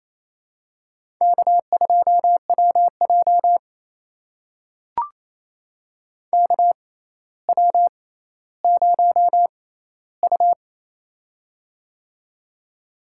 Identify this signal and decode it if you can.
Morse code, heard as a keyed tone, about 28 wpm, spaced as for 14 wpm, 700 Hz. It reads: K2WJ KW0U
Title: USA Call Signs - 4 Characters - 6x Character Spacing - 28wpm